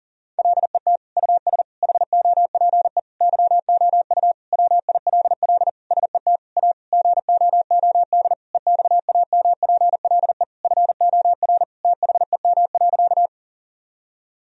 Morse code, heard as a keyed tone, about 40 wpm, 700 Hz. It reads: LET US HOPE YOU WILL SET A GOOD EXAMPLE FOR THEM.
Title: Sentences from Top 1000 Words - Review Part 1 - 40wpm